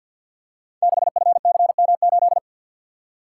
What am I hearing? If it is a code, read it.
Morse code, heard as a keyed tone, about 52 wpm, 700 Hz. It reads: 6FCM8
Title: Code Groups - 5 Letter:Numbers - 50wpm